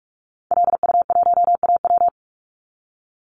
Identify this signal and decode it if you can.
Morse code, heard as a keyed tone, about 46 wpm, 700 Hz. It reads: LU1AW
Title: Call Signs - 45wpm